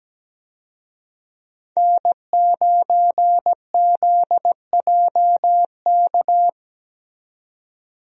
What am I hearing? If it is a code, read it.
Morse code, heard as a keyed tone, about 17 wpm, 700 Hz. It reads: N9ZJK
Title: USA Call Signs - 5 Characters - 17wpm